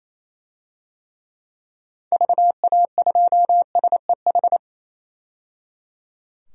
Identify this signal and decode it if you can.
Morse code, heard as a keyed tone, about 28 wpm, 700 Hz. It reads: VA2SEH
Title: DXCC-100 Call Signs - 6 Characters - 28wpm